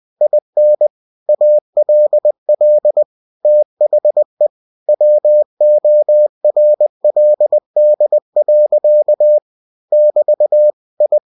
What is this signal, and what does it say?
Morse code, heard as a keyed tone, about 20 wpm, 600 Hz. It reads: IN ALL THE WORLD. <BT> I
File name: cw-books / winnie-the-pooh / winnie-20wpm-page-0108